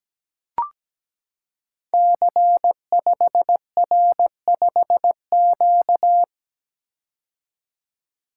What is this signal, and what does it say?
Morse code, heard as a keyed tone, about 17 wpm, 700 Hz. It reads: C5R5Q
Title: Code Groups - 5 Letter:Numbers - 17wpm